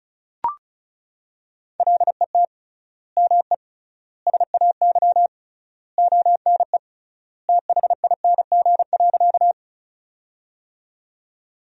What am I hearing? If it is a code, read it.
Morse code, heard as a keyed tone, about 36 wpm, 700 Hz. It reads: LET ME SAY ONE THING.
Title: 3x Word-Spacing - Sentences from Top 300 Words - 35wpm